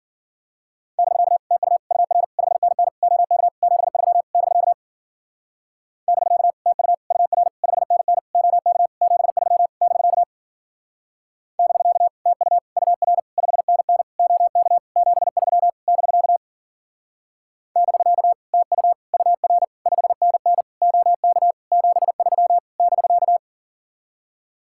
Morse code, heard as a keyed tone, about 49 wpm, 700 Hz. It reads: <BK> TU UR 5NN OK 73 <BK> <BK> TU UR 5NN OK 73 <BK> <BK> TU UR 5NN OK 73 <BK> <BK> TU UR 5NN OK 73 <BK>
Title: Speed-Racing Common QSO Phrases - 40wpm